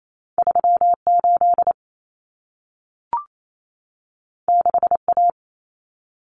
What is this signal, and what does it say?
Morse code, heard as a keyed tone, about 28 wpm, 700 Hz. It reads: 38 6A
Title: Common Cut Numbers - 2-Digit 28wpm